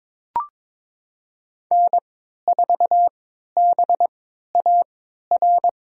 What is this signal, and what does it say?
Morse code, heard as a keyed tone, about 23 wpm, spaced as for 13 wpm, 700 Hz. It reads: N4BAR
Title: USA Call Signs - 5 Characters - 3x Character Spacing - 22wpm